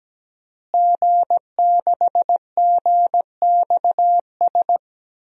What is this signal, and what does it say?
Morse code, heard as a keyed tone, about 17 wpm, 700 Hz. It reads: G6GXS
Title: DXCC-100 Call Signs - 5 Characters - Repeated 2x - 17wpm